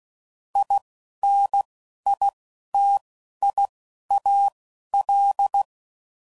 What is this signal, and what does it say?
Morse code, heard as a keyed tone, about 16 wpm, spaced as for 11 wpm, 800 Hz. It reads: INITIAL
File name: CW Code Groups @10WPM / Technical Subjects / Gates Intro